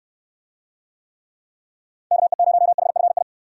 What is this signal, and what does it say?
Morse code, heard as a keyed tone, about 64 wpm, 700 Hz. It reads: KE0HFI